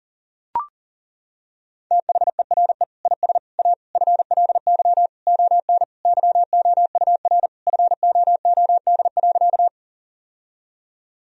Morse code, heard as a keyed tone, about 40 wpm, 700 Hz. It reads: THERE IS A FLY ON YOUR FOOD.
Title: Sentences from Top 1000 Words - Review Part 2 - 40wpm